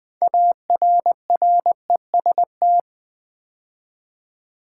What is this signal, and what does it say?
Morse code, heard as a keyed tone, about 20 wpm, 700 Hz. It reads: ARREST
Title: 6 Letter Words - First 3 Letters Given - 20wpm